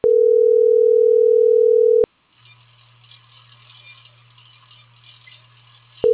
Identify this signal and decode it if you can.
Ringback tone (North America), 440 + 480 Hz.